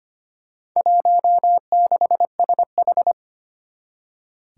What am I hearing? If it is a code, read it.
Morse code, heard as a keyed tone, about 25 wpm, 700 Hz. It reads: 16SH
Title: Four Challenging Letter-Number Combinations 25wpm